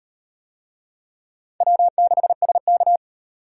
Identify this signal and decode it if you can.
Morse code, heard as a keyed tone, about 33 wpm, 700 Hz. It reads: W6SK